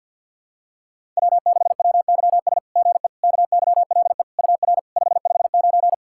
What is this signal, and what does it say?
Morse code, heard as a keyed tone, about 52 wpm, 690 Hz. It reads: W6WYS GE KYLE UR 559
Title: SOTA QSO Practice 50wpm